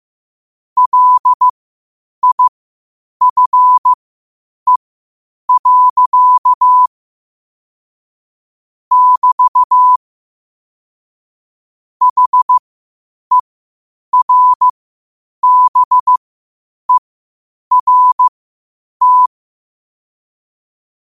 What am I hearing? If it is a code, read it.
Morse code, heard as a keyed tone, about 15 wpm, spaced as for 8 wpm, 1000 Hz. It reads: LIFE. <BT> HERBERT
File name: qotdfiles / 8wpm